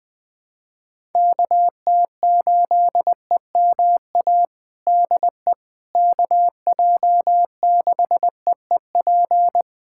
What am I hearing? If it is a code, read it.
Morse code, heard as a keyed tone, about 20 wpm, 700 Hz. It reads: KT8EMA DE KJ6EEP
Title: USA Callsign DE Callsign 20wpm